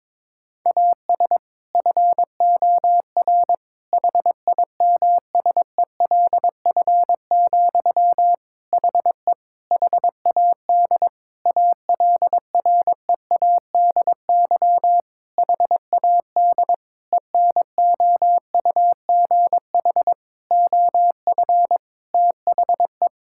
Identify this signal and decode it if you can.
Morse code, heard as a keyed tone, about 22 wpm, 700 Hz. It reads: AS FOR HIMSELF, HE HAD ALREADY HAD ENOUGH OF THE
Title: Sentences from Top 1600 Words 22wpm